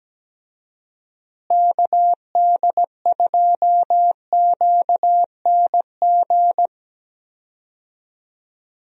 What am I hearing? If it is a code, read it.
Morse code, heard as a keyed tone, about 17 wpm, 700 Hz. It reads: KD2QNG
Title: USA Call Signs - 6 Characters - Repeated 3x - 17wpm